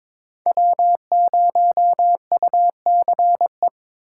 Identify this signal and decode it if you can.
Morse code, heard as a keyed tone, about 22 wpm, 700 Hz. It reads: W0UCE
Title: K1USN SST QSO Practice - 22wpm